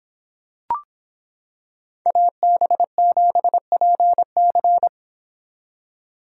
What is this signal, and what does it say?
Morse code, heard as a keyed tone, about 25 wpm, 700 Hz. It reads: AB7PC